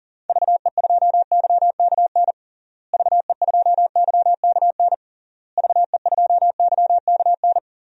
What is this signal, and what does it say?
Morse code, heard as a keyed tone, about 40 wpm, 700 Hz. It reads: VE2YKN VE2YKN VE2YKN
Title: DXCC-100 Call Signs - 6 Characters - Repeated 3x - 40wpm